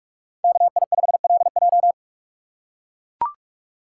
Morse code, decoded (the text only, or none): KI5LJ